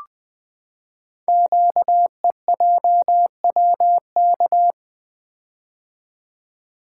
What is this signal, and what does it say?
Morse code, heard as a keyed tone, about 20 wpm, 700 Hz. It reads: QEJWK
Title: Code Groups - 5 Letters - 20wpm